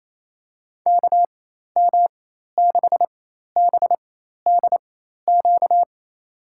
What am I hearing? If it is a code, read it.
Morse code, heard as a keyed tone, about 28 wpm, spaced as for 13 wpm, 700 Hz. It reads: KM6BDQ
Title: USA Call Signs - 6 Characters - 4x Character Spacing - 28wpm